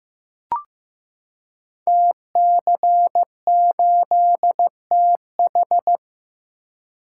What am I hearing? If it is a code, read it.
Morse code, heard as a keyed tone, about 15 wpm, 700 Hz. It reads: TC8TH
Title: Code Groups - 5 Letter:Numbers - 15wpm